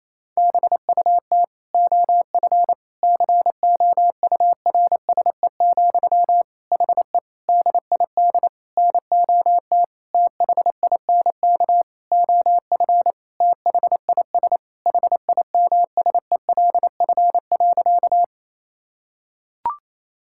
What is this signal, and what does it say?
Morse code, heard as a keyed tone, about 28 wpm, 700 Hz. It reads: BUT OF COURSE, HE DID NOT THINK OF THIS HIMSELF.
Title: Sentences from Top 1600 Words 28wpm